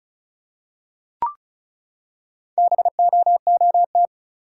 Morse code, heard as a keyed tone, about 36 wpm, 700 Hz. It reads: BOOT
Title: Four Letter Words - 35wpm